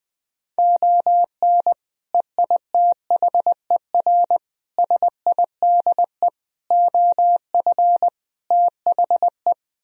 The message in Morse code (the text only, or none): ON EITHER SIDE OF THE